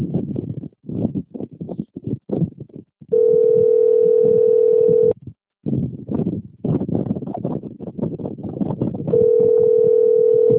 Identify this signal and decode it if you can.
Ringback tone (North America), 440 + 480 Hz.
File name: out- - - - - .39360